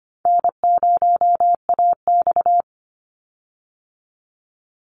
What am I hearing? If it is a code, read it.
Morse code, heard as a keyed tone, about 25 wpm, 700 Hz. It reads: N0AX
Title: Call Signs Repeated 3x - 25wpm